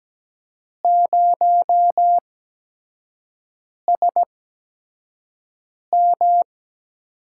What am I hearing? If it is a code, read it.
Morse code, heard as a keyed tone, about 17 wpm, spaced as for 5 wpm, 700 Hz. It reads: 0SM
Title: USA Call Signs - 6 Characters - 8x Character Spacing - 17wpm